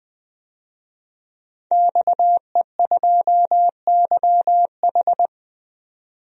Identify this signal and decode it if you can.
Morse code, heard as a keyed tone, about 20 wpm, 700 Hz. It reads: XE2YH
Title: DXCC-100 Call Signs - 5 Characters - 20wpm